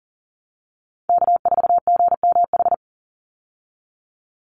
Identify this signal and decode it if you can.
Morse code, heard as a keyed tone, about 39 wpm, 700 Hz. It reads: K4GMH
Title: Call Signs Repeated 2x - 40wpm